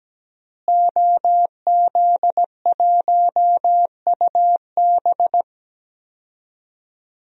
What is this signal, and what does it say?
Morse code, heard as a keyed tone, about 17 wpm, 700 Hz. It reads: OZ1UB
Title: DXCC-100 Call Signs - 5 Characters - Repeated 2x - 17wpm